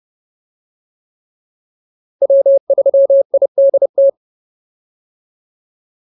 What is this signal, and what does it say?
Morse code, heard as a keyed tone, about 30 wpm, 550 Hz. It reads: W3IDT